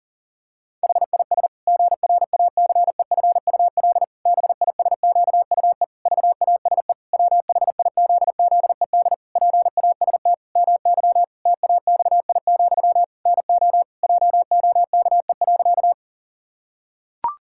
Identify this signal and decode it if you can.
Morse code, heard as a keyed tone, about 40 wpm, 700 Hz. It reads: HIS GRACEFUL BISQUE VASE WHIZZED PAST MY TAXI, NO JOKE.
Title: Pangram Challenge v2 - 40wpm